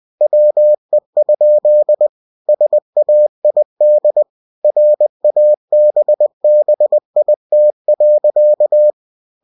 Morse code, heard as a keyed tone, about 20 wpm, 600 Hz. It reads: WE? SAID RABBIT.